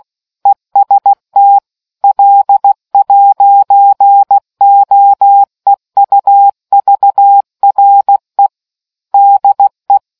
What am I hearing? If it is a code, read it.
Morse code, heard as a keyed tone, about 16 wpm, 800 Hz. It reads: EST L'OEUVRE DE